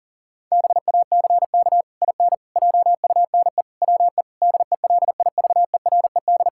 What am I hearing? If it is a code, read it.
Morse code, heard as a keyed tone, about 40 wpm, 700 Hz. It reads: BACK IN JUNE WE DELIVERED
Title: Pangram Challenge v2 - 40wpm